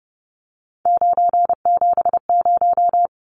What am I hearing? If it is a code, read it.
Morse code, heard as a keyed tone, about 30 wpm, 700 Hz. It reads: 970
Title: Numbers - 3-Digit 30wpm